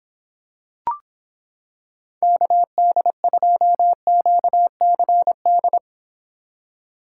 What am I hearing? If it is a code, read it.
Morse code, heard as a keyed tone, about 26 wpm, 700 Hz. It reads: KD2QCD